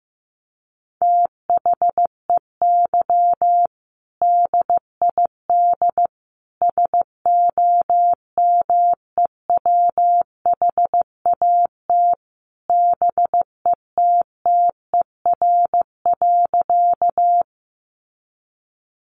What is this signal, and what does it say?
Morse code, heard as a keyed tone, about 15 wpm, 700 Hz. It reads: THEY DID SOMEWHAT BETTER.
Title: Sentences from Top 1300 Words 15wpm